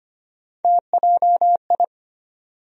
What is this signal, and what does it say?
Morse code, heard as a keyed tone, about 25 wpm, 700 Hz. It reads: TJI
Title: USA Call Sign Postfixes - 3 Letters 25wpm